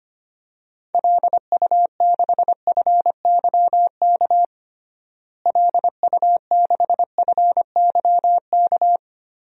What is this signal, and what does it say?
Morse code, heard as a keyed tone, about 25 wpm, 700 Hz. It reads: LU6FYK LU6FYK
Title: DXCC-100 Call Signs - 6 Characters - Repeated 3x - 25wpm